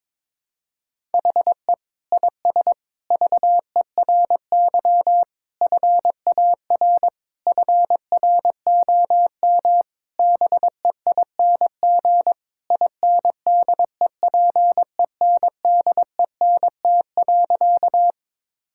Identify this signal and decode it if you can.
Morse code, heard as a keyed tone, about 21 wpm, 700 Hz. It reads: HE IS VERY FAR FROM BEING INDEPENDENT.
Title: Sentences from Top 1000 Words - Review Part 6 - 22wpm